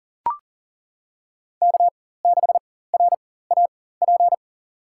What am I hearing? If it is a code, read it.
Morse code, heard as a keyed tone, about 38 wpm, spaced as for 18 wpm, 700 Hz. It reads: K6RAP
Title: USA Call Signs - 4x Character Spacing - 40wpm